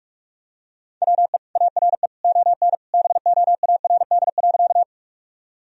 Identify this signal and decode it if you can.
Morse code, heard as a keyed tone, about 46 wpm, 700 Hz. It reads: WE ARE ON BOARD.
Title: Sentences from Top 1000 Words - Review Part 4 - 45wpm